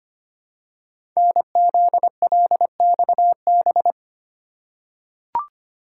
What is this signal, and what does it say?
Morse code, heard as a keyed tone, about 25 wpm, 700 Hz. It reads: NZLXB
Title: Code Groups - 5 Letters - 25wpm